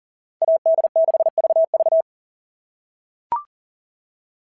Morse code, heard as a keyed tone, about 38 wpm, 660 Hz. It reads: AD6VV
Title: SOTA QSO Practice 40wpm